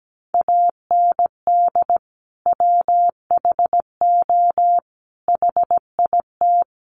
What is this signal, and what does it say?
Morse code, heard as a keyed tone, about 17 wpm, 700 Hz. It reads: AND WHO HIT
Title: Sentences from Top 1200 Words 17wpm